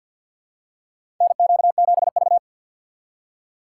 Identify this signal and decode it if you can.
Morse code, heard as a keyed tone, about 53 wpm, 700 Hz. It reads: NQ7V
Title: USA Call Signs - 4 Characters - 6x Character Spacing - 50wpm